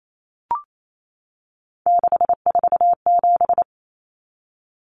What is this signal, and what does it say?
Morse code, heard as a keyed tone, about 28 wpm, 700 Hz. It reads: 647
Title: Numbers - 3-Digit 28wpm